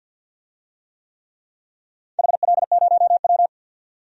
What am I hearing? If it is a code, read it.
Morse code, heard as a keyed tone, about 51 wpm, 700 Hz. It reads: HL0W
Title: DXCC-100 Call Signs - 4 Characters - 50wpm